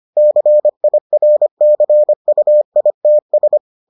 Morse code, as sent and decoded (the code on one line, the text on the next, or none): -.-. .. .-. -.-. ..- .. - ...
CIRCUITS